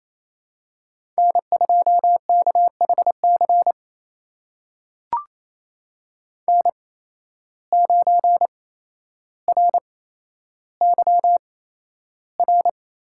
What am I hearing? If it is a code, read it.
Morse code, heard as a keyed tone, about 28 wpm, spaced as for 12 wpm, 700 Hz. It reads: N2KHC N9RYR
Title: USA Call Signs - 5 Characters - 8x Character Spacing - 28wpm